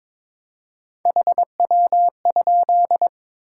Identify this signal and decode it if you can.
Morse code, heard as a keyed tone, about 21 wpm, 700 Hz. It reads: HW?
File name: Top 20 QSO Elements - 4x Character Spacing - 22wpm